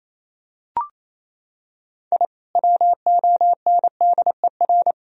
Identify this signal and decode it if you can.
Morse code, heard as a keyed tone, about 28 wpm, 700 Hz. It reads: I WONDER
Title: Sentences from Top 1000 Words - Review Part 3 - 28wpm